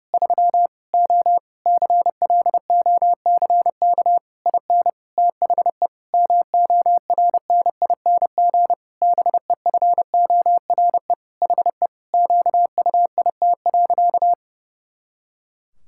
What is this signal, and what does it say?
Morse code, heard as a keyed tone, about 28 wpm, 700 Hz. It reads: 3 O CLOCK IN THE MORNING BEFORE HE QUIT.
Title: Pangram Challenge v2 - 30wpm